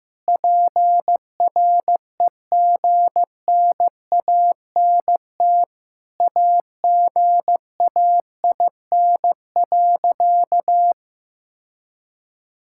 Morse code, heard as a keyed tone, about 15 wpm, 700 Hz. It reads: PREGNANT AGAIN.
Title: Sentences from Top 1000 Words - Review Part 6 - 15wpm